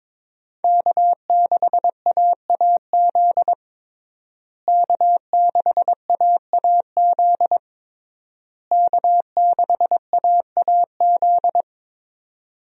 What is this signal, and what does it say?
Morse code, heard as a keyed tone, about 22 wpm, 700 Hz. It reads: K6AAZ K6AAZ K6AAZ
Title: USA Call Signs - 5 Characters - Repeated 3x - 22wpm